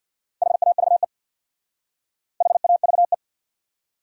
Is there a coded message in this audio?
Yes: Morse code from 0.4 to 3.1 s, heard as a keyed tone, about 66 wpm, 700 Hz. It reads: HAVE HAVE